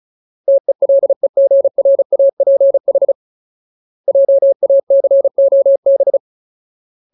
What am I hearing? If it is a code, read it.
Morse code, heard as a keyed tone, about 34 wpm, 550 Hz. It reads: TELEGRAPH JACOB
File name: Ultimate Word List / 35WPM Ultimate